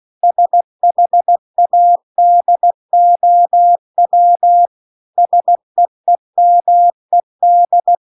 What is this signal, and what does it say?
Morse code, heard as a keyed tone, about 16 wpm, 700 Hz. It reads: SHADOW SEEMED
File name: CW-The-Big-Twoheaded-River-1-16WPM